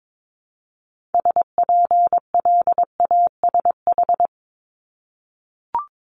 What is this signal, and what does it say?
Morse code, heard as a keyed tone, about 22 wpm, 700 Hz. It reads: SPLASH
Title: Multi-letter Phonemes 22wpm